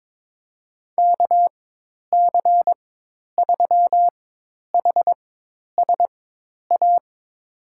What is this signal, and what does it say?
Morse code, heard as a keyed tone, about 22 wpm, spaced as for 10 wpm, 700 Hz. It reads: KC3HSA
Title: USA Call Signs - 6 Characters - 4x Character Spacing - 22wpm